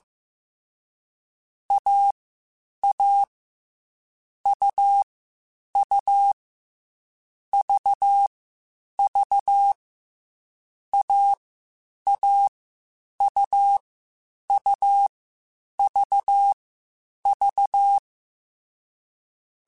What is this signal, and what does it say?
Morse code, heard as a keyed tone, about 15 wpm, 800 Hz. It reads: A A U U V V A A U U V V